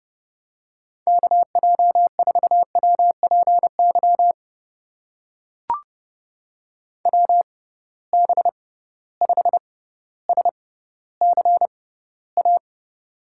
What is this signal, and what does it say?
Morse code, heard as a keyed tone, about 30 wpm, spaced as for 15 wpm, 700 Hz. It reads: KJ4WPY WB5SCA
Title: USA Call Signs - 6 Characters - 6x Character Spacing - 30wpm